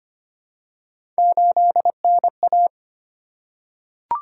Morse code, heard as a keyed tone, about 25 wpm, 700 Hz. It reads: 8NA
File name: Three Letter-Number Combinations - 25wpm